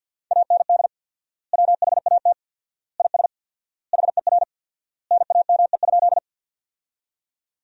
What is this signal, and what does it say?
Morse code, heard as a keyed tone, about 50 wpm, 700 Hz. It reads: AND WHAT IS HER NAME?
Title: 4x Word-Spacing - Sentences from Top 200 Words - 50wpm